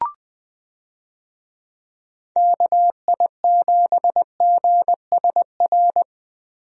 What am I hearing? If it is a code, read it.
Morse code, heard as a keyed tone, about 20 wpm, 700 Hz. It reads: KI7GSR
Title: USA Call Signs - 6 Characters - Repeated 3x - 20wpm